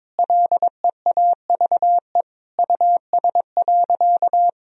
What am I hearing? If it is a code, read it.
Morse code, heard as a keyed tone, about 22 wpm, 700 Hz. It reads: LEAVE US.
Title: Sentences from Top 1000 Words - Review Part 3 - 22wpm